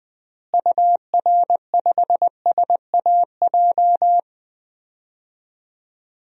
Morse code, heard as a keyed tone, about 20 wpm, 700 Hz. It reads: UR5SAJ
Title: DXCC-100 Call Signs - 6 Characters - Repeated 3x - 20wpm